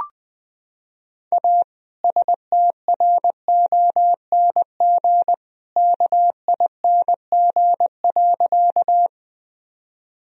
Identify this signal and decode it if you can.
Morse code, heard as a keyed tone, about 20 wpm, 700 Hz. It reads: A STRONG KING.
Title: Sentences from Top 1000 Words - Review Part 3 - 20wpm